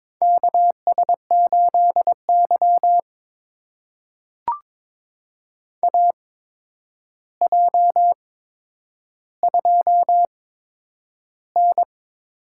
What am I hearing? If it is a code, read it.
Morse code, heard as a keyed tone, about 22 wpm, spaced as for 9 wpm, 700 Hz. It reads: KS8Y AJ2N
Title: USA Call Signs - 4 Characters - 8x Character Spacing - 22wpm